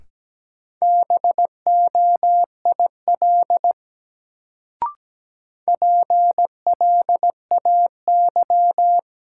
Morse code, heard as a keyed tone, about 17 wpm, 700 Hz. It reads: BOIL PLAY
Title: Sets of 5 Words Encore - New Word Drill - 17wpm